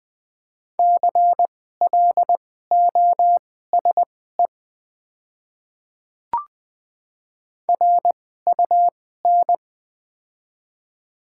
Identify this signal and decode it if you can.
Morse code, heard as a keyed tone, about 20 wpm, spaced as for 14 wpm, 700 Hz. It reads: CLOSE RUN
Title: 2x Character Spacing - Top 300 Words - 20wpm